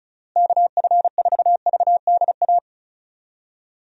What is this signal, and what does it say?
Morse code, heard as a keyed tone, about 36 wpm, 700 Hz. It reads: KF4VDA
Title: USA Call Signs - 6 Characters - Repeated 2x - 35wpm